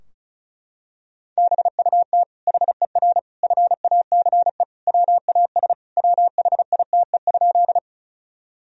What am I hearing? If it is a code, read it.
Morse code, heard as a keyed tone, about 36 wpm, 700 Hz. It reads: BUT HER FACE WAS WHITE?
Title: Sentences from Top 1000 Words - Review Part 2 - 35wpm